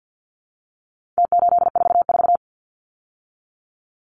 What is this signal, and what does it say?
Morse code, heard as a keyed tone, about 50 wpm, 700 Hz. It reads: T844
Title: Numbers - Contest Style - 4-Digit 50wpm